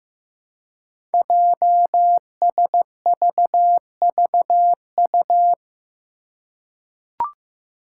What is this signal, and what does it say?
Morse code, heard as a keyed tone, about 15 wpm, 700 Hz. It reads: JSVVU
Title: Five Challenging Letter-Number Combinations 15wpm